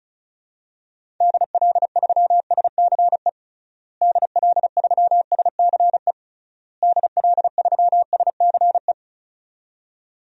Morse code, heard as a keyed tone, about 36 wpm, 700 Hz. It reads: DL3SCE DL3SCE DL3SCE
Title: DXCC-100 Call Signs - 6 Characters - Repeated 3x - 35wpm